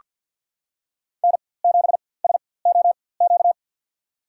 Morse code, heard as a keyed tone, about 46 wpm, spaced as for 23 wpm, 700 Hz. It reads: N7SOQ
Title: USA Call Signs - 4x Character Spacing - 50wpm